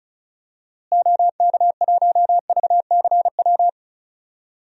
Morse code, heard as a keyed tone, about 34 wpm, 700 Hz. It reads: OK1VCW
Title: DXCC-100 Call Signs - 6 Characters - Repeated 3x - 35wpm